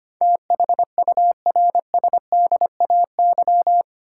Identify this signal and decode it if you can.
Morse code, heard as a keyed tone, about 25 wpm, 700 Hz. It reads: THURSDAY
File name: Eight Letter Words - 25wpm